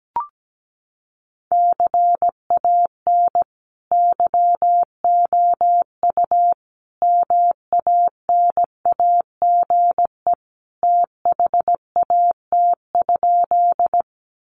Morse code, heard as a keyed tone, about 17 wpm, 700 Hz. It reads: CAN YOU MANAGE THAT?
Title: Sentences from Top 1400 Words 17wpm